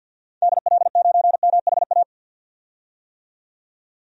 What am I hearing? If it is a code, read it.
Morse code, heard as a keyed tone, about 46 wpm, 700 Hz. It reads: DL9MHA
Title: DXCC-100 Call Signs - 6 Characters - 50wpm